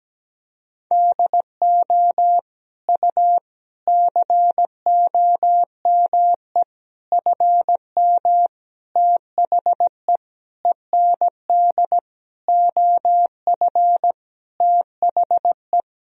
Morse code, heard as a keyed tone, about 17 wpm, 700 Hz. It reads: DO U COME FM THE END OF THE
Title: Sentences from Top 200 Words and Abbreviations 17wpm